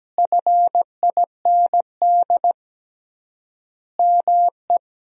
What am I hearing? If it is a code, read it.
Morse code, heard as a keyed tone, about 17 wpm, 700 Hz. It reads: FIND ME
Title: 3x Word-Spacing - Sentences from Top 500 Words - 17wpm